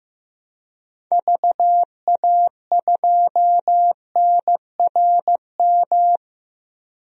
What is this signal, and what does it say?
Morse code, heard as a keyed tone, about 15 wpm, 700 Hz. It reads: VA2NRM
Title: DXCC-100 Call Signs - 6 Characters - 15wpm